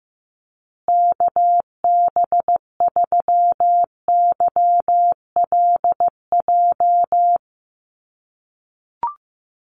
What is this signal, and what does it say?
Morse code, heard as a keyed tone, about 15 wpm, 700 Hz. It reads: KB3YLJ
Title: USA Call Signs Repeated 3x 15wpm